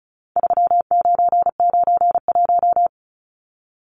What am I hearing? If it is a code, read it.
Morse code, heard as a keyed tone, about 34 wpm, 700 Hz. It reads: 3991
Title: Numbers - 4-Digit 35wpm